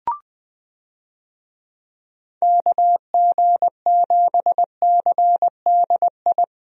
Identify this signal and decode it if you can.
Morse code, heard as a keyed tone, about 21 wpm, 700 Hz. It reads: KG7CDI
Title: USA Call Signs - 6 Characters - 20wpm